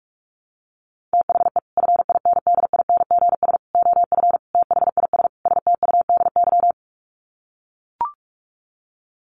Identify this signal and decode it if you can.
Morse code, heard as a keyed tone, about 44 wpm, 700 Hz. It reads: THE FINDINGS OF THIS STUDY
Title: Sets of 5 Words Encore2 45wpm